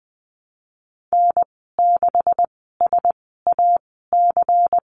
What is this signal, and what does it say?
Morse code, heard as a keyed tone, about 20 wpm, spaced as for 15 wpm, 700 Hz. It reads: N6SAC
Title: 161 Lesson 26g - Letter g - Call Signs 20wpm